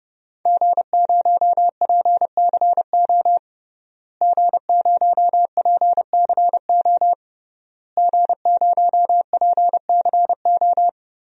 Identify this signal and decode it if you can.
Morse code, heard as a keyed tone, about 30 wpm, 700 Hz. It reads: G0PCO G0PCO G0PCO
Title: DXCC-100 Call Signs - 5 Characters - Repeated 3x - 30wpm